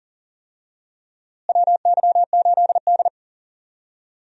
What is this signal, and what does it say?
Morse code, heard as a keyed tone, about 38 wpm, 700 Hz. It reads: WY8D
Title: USA Call Signs - 4 Characters - Repeated 3x - 40wpm